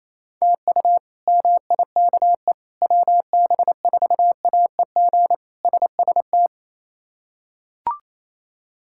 Morse code, heard as a keyed tone, about 28 wpm, 700 Hz. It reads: TU MIKE WB4AEG SST
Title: K1USN SST QSO Practice - 28wpm